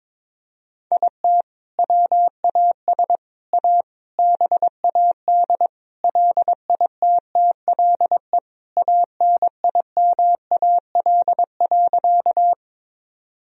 Morse code, heard as a keyed tone, about 22 wpm, 700 Hz. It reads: IT WAS A BAD LITTLE ANIMAL.
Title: Sentences from Top 1000 Words - Review Part 1 - 22wpm